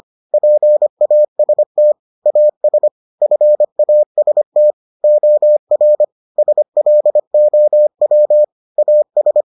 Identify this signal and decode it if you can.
Morse code, heard as a keyed tone, about 25 wpm, 600 Hz. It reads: PAST AS FAST OR SLOW AS